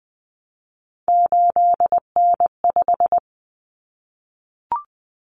Morse code, heard as a keyed tone, about 20 wpm, 700 Hz. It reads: 8N5
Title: Common Cut Numbers - 3-Digit 20wpm